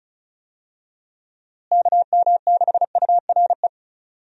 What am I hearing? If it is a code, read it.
Morse code, heard as a keyed tone, about 34 wpm, 700 Hz. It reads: KM6URE
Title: USA Call Signs - 6 Characters - Repeated 3x - 35wpm